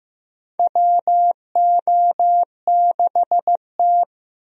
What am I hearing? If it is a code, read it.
Morse code, heard as a keyed tone, about 15 wpm, 700 Hz. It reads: WO6T
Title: USA Call Signs - 4 Characters - Repeated 3x - 15wpm